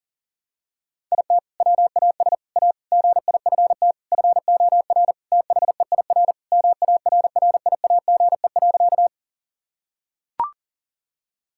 Morse code, heard as a keyed tone, about 40 wpm, 700 Hz. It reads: IT WAS A GIFT FOR THEIR MARRIAGE.
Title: Sentences from Top 1000 Words - Review Part 6 - 40wpm